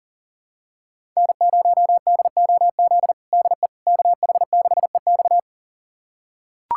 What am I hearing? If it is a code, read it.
Morse code, heard as a keyed tone, about 40 wpm, 700 Hz. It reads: N0DOZ DE KH6EX
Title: USA Callsign DE Callsign - 5 Characters - 40wpm